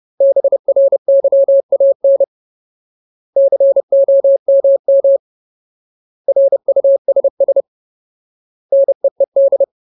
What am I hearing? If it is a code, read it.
Morse code, heard as a keyed tone, about 30 wpm, 550 Hz. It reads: BRYAN COMM RUSS NEED
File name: Ultimate Word List / 30WPM Ultimate